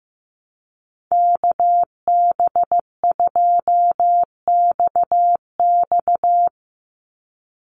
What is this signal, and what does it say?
Morse code, heard as a keyed tone, about 15 wpm, 700 Hz. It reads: KB2XX
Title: NA SOTA Chaser Callsigns 15wpm